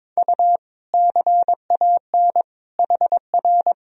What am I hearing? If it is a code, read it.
Morse code, heard as a keyed tone, about 22 wpm, 700 Hz. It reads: U CAN HR
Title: Sentences from Top 500 Words and Abbreviations 22wpm